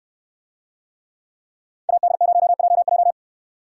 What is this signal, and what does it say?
Morse code, heard as a keyed tone, about 69 wpm, 700 Hz. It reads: AD0JJ